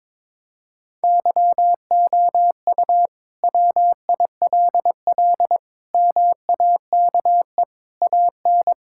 Morse code, heard as a keyed tone, about 21 wpm, 700 Hz. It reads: YOU WILL MAKE AN
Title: Sentences from Top 1000 Words - Review Part 5 - 22wpm